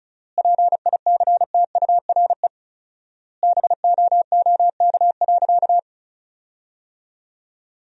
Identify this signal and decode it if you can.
Morse code, heard as a keyed tone, about 34 wpm, 700 Hz. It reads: PICTURE BOOK.
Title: 4x Word-Spacing - Sentences from Top 500 Words - 35wpm